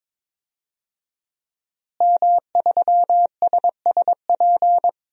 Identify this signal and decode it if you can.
Morse code, heard as a keyed tone, about 22 wpm, 700 Hz. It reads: M3SSP
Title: DXCC-100 Call Signs - 5 Characters - Repeated 3x - 22wpm